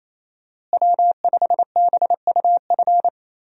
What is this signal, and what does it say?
Morse code, heard as a keyed tone, about 28 wpm, 700 Hz. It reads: W5BUF
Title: USA Call Signs - 5 Characters - Repeated 3x - 28wpm